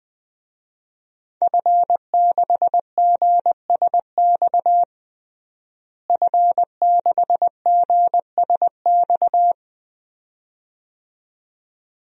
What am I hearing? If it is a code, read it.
Morse code, heard as a keyed tone, about 21 wpm, 700 Hz. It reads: F6GSX F6GSX
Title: DXCC-100 Call Signs - 5 Characters - Repeated 2x - 20wpm